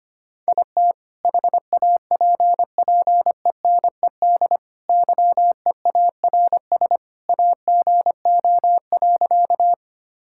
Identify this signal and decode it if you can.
Morse code, heard as a keyed tone, about 25 wpm, 700 Hz. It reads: IT HAPPENED YEARS AGO.
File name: practice / 25wpm / Sentences with 2-Word Phrases - 25wpm